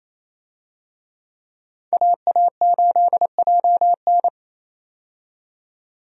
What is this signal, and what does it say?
Morse code, heard as a keyed tone, about 27 wpm, 700 Hz. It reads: AA8JN
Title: USA Call Signs - 5 Characters - 28wpm